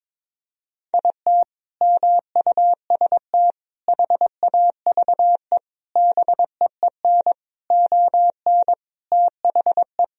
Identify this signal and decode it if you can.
Morse code, heard as a keyed tone, about 22 wpm, 700 Hz. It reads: IT MUST HAVE BEEN ON THE
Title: Sentences from Top 1000 Words - Review Part 1 - 22wpm